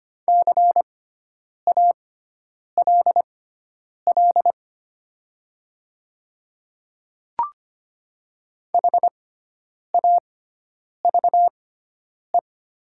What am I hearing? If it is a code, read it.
Morse code, heard as a keyed tone, about 25 wpm, spaced as for 9 wpm, 700 Hz. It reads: CALL HAVE
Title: 6x Character Spacing - Top 500 Words - 25wpm